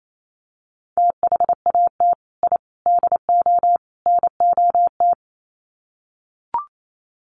Morse code, heard as a keyed tone, about 28 wpm, 700 Hz. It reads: THAT I DO NOT
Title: Sets of 4 Words 28wpm